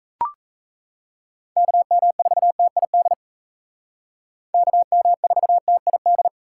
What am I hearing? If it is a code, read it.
Morse code, heard as a keyed tone, about 36 wpm, 700 Hz. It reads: KM4TID KM4TID